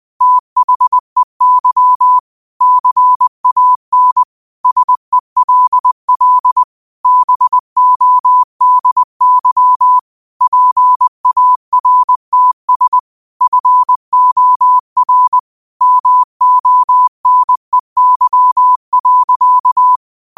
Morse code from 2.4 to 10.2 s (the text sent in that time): CAN SELL BODY